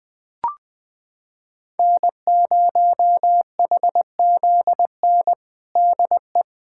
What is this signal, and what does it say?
Morse code, heard as a keyed tone, about 21 wpm, 700 Hz. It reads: N0HZN DE
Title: USA Callsign DE Callsign - 5 Characters - 20wpm